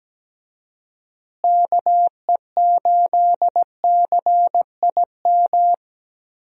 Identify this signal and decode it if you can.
Morse code, heard as a keyed tone, about 17 wpm, 700 Hz. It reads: KE8CIM
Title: USA Call Signs - 6 Characters - Repeated 3x - 17wpm